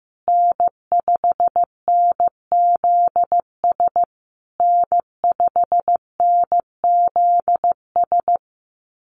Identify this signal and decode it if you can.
Morse code, heard as a keyed tone, about 15 wpm, 700 Hz. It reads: N5NZS N5NZS
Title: USA Call Signs Repeated 3x 15wpm